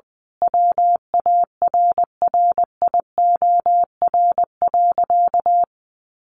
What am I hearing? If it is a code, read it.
Morse code, heard as a keyed tone, about 20 wpm, 700 Hz. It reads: WARRIOR.
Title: A Sound Of Thunder 20wpm